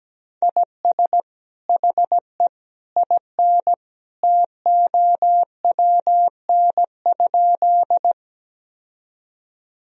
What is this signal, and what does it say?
Morse code, heard as a keyed tone, about 17 wpm, 700 Hz. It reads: IS HE IN TOWN?
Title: Sentences from Top 1000 Words - Review Part 2 - 17wpm